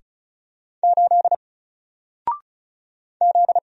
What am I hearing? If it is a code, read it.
Morse code, heard as a keyed tone, about 35 wpm, 700 Hz. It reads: 8 Z
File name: Single-Letter-Number-35wpm